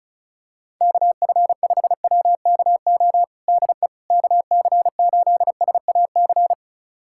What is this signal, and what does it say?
Morse code, heard as a keyed tone, about 36 wpm, 700 Hz. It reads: KF5WKO DE KC8SAC
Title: USA Callsign DE Callsign 35wpm